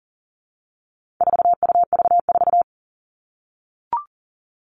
Morse code, heard as a keyed tone, about 40 wpm, 700 Hz. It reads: 4UV4
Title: All Cut Numbers - 4-Digit 40wpm